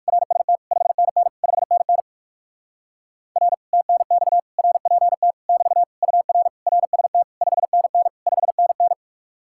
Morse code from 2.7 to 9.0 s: R TNX RPT <BT> UR RST 5NN 5NN

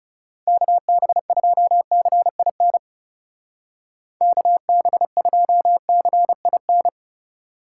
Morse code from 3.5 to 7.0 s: KB2CIN

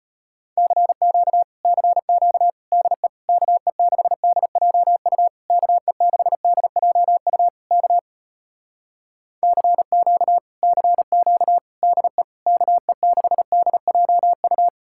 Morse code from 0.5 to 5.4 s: CQ CQ DE KE6DJU

